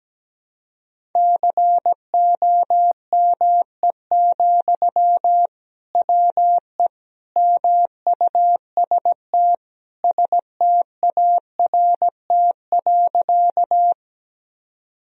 Morse code, heard as a keyed tone, about 17 wpm, 700 Hz. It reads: COME, WE MUST START.
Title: Sentences from Top 1000 Words - Review Part 1 - 17wpm